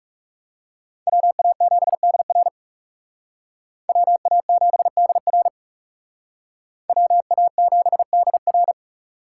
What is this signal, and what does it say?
Morse code, heard as a keyed tone, about 34 wpm, 700 Hz. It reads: WA7DR WA7DR WA7DR
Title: Speed Racing - USA 2x2 Call Signs 30wpm